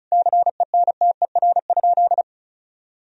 Morse code, heard as a keyed tone, about 36 wpm, 700 Hz. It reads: CENTER?
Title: Sentences from Top 1000 Words - Review Part 2 - 35wpm